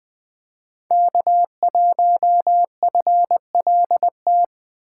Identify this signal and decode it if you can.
Morse code, heard as a keyed tone, about 20 wpm, 700 Hz. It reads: K1FLT
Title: USA Call Signs - 5 Characters - 2x Character Spacing - 20wpm